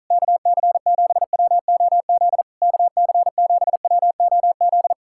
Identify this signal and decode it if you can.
Morse code, heard as a keyed tone, about 46 wpm, 700 Hz. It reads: KC7WOZ KC7WOZ